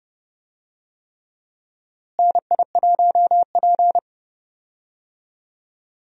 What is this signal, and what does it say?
Morse code, heard as a keyed tone, about 30 wpm, 700 Hz. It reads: NI1P
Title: USA Call Signs - 4 Characters - Repeated 3x - 30wpm